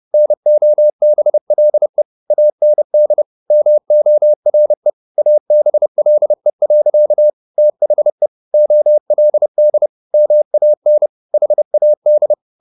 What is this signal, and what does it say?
Morse code, heard as a keyed tone, about 30 wpm, 600 Hz. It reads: NOBLE AND MORE ABLE. THE OLD MAN HAD
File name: the old man and the sea / the old man and the sea 1 30wpm